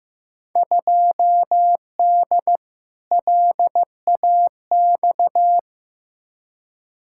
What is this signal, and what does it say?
Morse code, heard as a keyed tone, about 15 wpm, 700 Hz. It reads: 2D LAX
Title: Field Day Exchanges - 15wpm